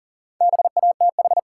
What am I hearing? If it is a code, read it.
Morse code, heard as a keyed tone, about 40 wpm, 700 Hz. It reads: BATH
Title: Sentences from Top 1600 Words - New Word Drill Alt - 40wpm